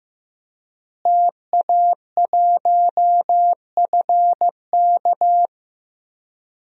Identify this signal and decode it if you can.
Morse code, heard as a keyed tone, about 15 wpm, 700 Hz. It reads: TA1FK
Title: DXCC-100 Call Signs - 5 Characters - Repeated 3x - 15wpm